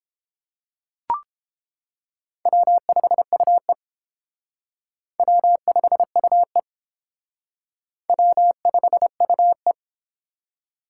Morse code, heard as a keyed tone, about 30 wpm, 700 Hz. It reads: W5UE W5UE W5UE